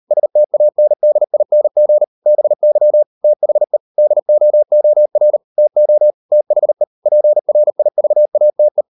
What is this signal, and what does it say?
Morse code, heard as a keyed tone, about 38 wpm, 600 Hz. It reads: STANDING BY THE DOOR TO THE PRIVATE